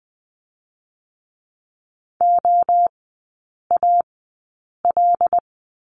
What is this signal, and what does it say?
Morse code, heard as a keyed tone, about 20 wpm, spaced as for 8 wpm, 700 Hz. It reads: OAL
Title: 074 Lesson 13e - Letter L - Three Characters 20wpm